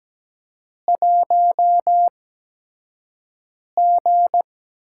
Morse code, heard as a keyed tone, about 17 wpm, spaced as for 5 wpm, 700 Hz. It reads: 1G
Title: USA Call Signs - 5 Characters - 8x Character Spacing - 17wpm